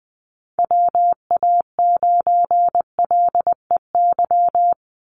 Morse code, heard as a keyed tone, about 20 wpm, 700 Hz. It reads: WA9LEY
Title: Call Signs Repeated 2x - 20wpm